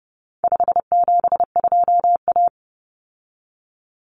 Morse code, heard as a keyed tone, about 30 wpm, 700 Hz. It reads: 572A
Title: Common Cut Numbers - 4-Digit 30wpm